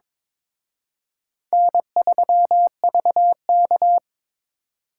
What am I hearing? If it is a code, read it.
Morse code, heard as a keyed tone, about 26 wpm, 700 Hz. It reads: N3VK
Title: Speed Racing - USA Call Signs 22wpm